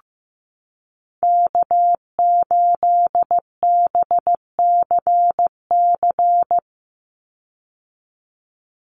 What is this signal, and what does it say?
Morse code, heard as a keyed tone, about 15 wpm, 700 Hz. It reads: K8BCC
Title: USA Call Signs 15wpm